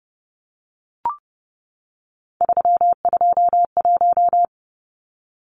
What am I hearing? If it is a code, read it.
Morse code, heard as a keyed tone, about 30 wpm, 700 Hz. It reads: 321
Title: Numbers - 3-Digit 30wpm